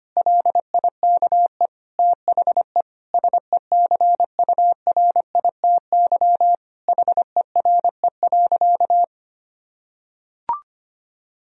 Morse code, heard as a keyed tone, about 25 wpm, 700 Hz. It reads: LIKE THE SECURITY HERE.
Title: Sentences from Top 1000 Words - Review Part 4 - 25wpm